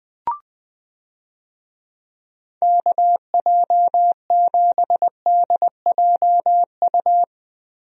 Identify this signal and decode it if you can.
Morse code, heard as a keyed tone, about 20 wpm, 700 Hz. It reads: KJ7DJU
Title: USA Call Signs - 6 Characters - Repeated 3x - 20wpm